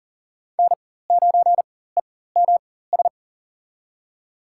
Morse code, heard as a keyed tone, about 38 wpm, spaced as for 18 wpm, 700 Hz. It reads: N9EMS
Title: USA Call Signs - 5 Characters - 4x Character Spacing - 40wpm